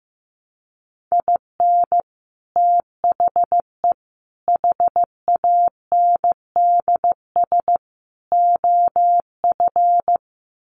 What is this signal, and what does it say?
Morse code, heard as a keyed tone, about 15 wpm, 700 Hz. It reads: IN THE HANDS OF
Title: Sets of 4 Words 15wpm